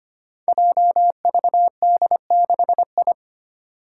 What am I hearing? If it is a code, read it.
Morse code, heard as a keyed tone, about 25 wpm, 700 Hz. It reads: JVD6I
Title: Five Challenging Letter-Number Combinations 25wpm